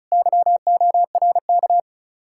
Morse code, heard as a keyed tone, about 40 wpm, 700 Hz. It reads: YORK